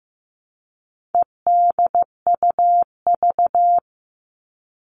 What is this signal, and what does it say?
Morse code, heard as a keyed tone, about 15 wpm, 700 Hz. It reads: EDUV
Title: All Cut Numbers - 4-Digit 15wpm